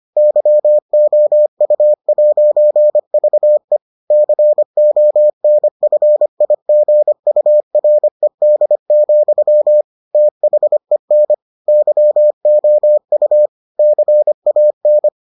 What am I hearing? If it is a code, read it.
Morse code, heard as a keyed tone, about 25 wpm, 600 Hz. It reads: YOU'VE CONFIGURED, THEN YOU CAN